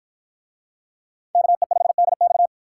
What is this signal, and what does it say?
Morse code, heard as a keyed tone, about 53 wpm, 700 Hz. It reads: KE5DX